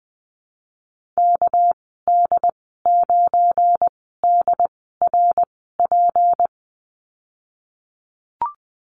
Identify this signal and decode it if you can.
Morse code, heard as a keyed tone, about 20 wpm, spaced as for 14 wpm, 700 Hz. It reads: KD9DRP
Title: 245 Lesson 39g - Letter z - Call Signs 20wpm